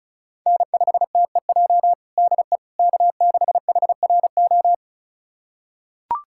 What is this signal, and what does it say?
Morse code, heard as a keyed tone, about 36 wpm, 690 Hz. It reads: N5TEJ DE K6HRO
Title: USA Callsign DE Callsign 35wpm